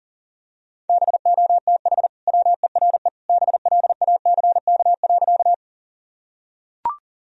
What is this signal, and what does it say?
Morse code, heard as a keyed tone, about 37 wpm, 700 Hz. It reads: BOTH WERE BLACK.
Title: Sentences from Top 1000 Words - Review Part 2 - 40wpm